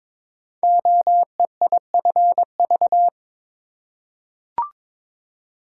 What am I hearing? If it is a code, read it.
Morse code, heard as a keyed tone, about 22 wpm, 700 Hz. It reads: OEIFV
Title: Code Groups - 5 Letters - 22wpm